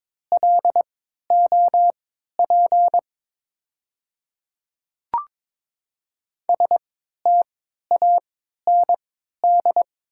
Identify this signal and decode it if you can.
Morse code, heard as a keyed tone, about 22 wpm, spaced as for 12 wpm, 700 Hz. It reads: LOP STAND
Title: 3x Character Spacing - Top 500 Words - 22wpm